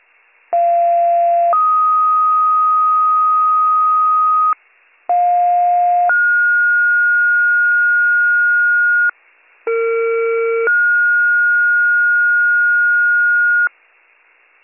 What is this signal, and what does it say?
Two-tone page, heard as a one-second tone, then a three-second tone, about 690 then 1250 Hz, about 710 then 1430 Hz, about 460 then 1430 Hz, with noise. It is heard from 0.5 to 13.7 s.